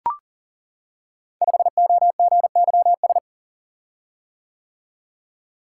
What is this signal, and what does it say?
Morse code, heard as a keyed tone, about 39 wpm, 700 Hz. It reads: 5OGYS
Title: Code Groups - 5 Letter:Numbers - 40wpm